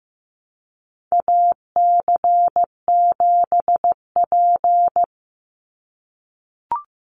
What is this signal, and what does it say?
Morse code, heard as a keyed tone, about 15 wpm, 700 Hz. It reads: AC7P
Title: NA SOTA Chaser Callsigns 15wpm